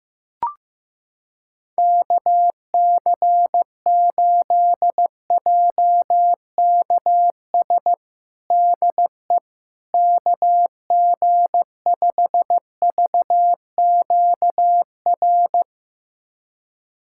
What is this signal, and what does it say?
Morse code, heard as a keyed tone, about 15 wpm, 700 Hz. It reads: KC8JKS DE KG5VQR
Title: USA Callsign DE Callsign - 6 Characters - 15wpm